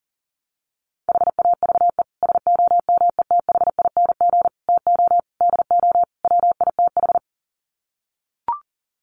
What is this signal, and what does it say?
Morse code, heard as a keyed tone, about 40 wpm, 700 Hz. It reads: HAVE SOMETHING TO DO WITH
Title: Sets of 5 Words Encore 40wpm